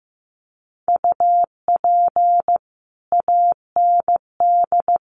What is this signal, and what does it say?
Morse code, heard as a keyed tone, about 15 wpm, 700 Hz. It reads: UP AND
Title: Sets of 2 Words 15wpm